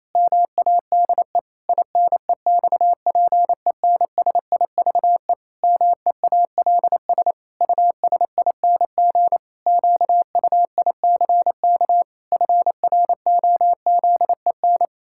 Morse code, heard as a keyed tone, about 28 wpm, 700 Hz. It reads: MADE INEXPENSIVE MEALS USING QUICK FROZEN
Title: Pangram Challenge v2 - 28wpm